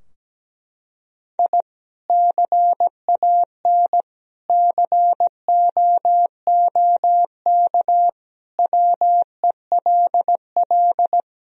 Morse code, heard as a keyed tone, about 17 wpm, 700 Hz. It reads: I CAN COOK WELL
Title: Sentences from Top 1000 Words - Review Part 4 - 17wpm